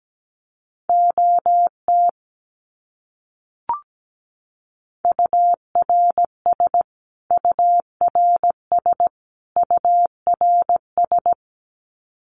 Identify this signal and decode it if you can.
Morse code, heard as a keyed tone, about 17 wpm, 700 Hz. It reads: OT URS URS URS
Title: Abbreviations Repeated 3x - 17wpm